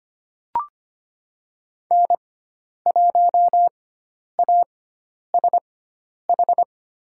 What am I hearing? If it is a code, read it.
Morse code, heard as a keyed tone, about 25 wpm, spaced as for 10 wpm, 700 Hz. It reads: N1ASH
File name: Lesson 16g - Letter c - Call Signs 25wpm